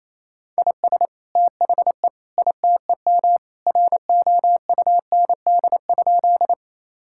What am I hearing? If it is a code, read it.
Morse code, heard as a keyed tone, about 28 wpm, 700 Hz. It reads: IS THE ITEM ROUND?
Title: Sentences from Top 1000 Words - Review Part 6 - 28wpm